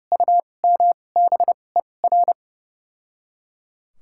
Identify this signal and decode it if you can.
Morse code, heard as a keyed tone, about 32 wpm, spaced as for 22 wpm, 700 Hz. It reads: UMBER
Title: 2x Character Spacing - Top 100 Words - 30wpm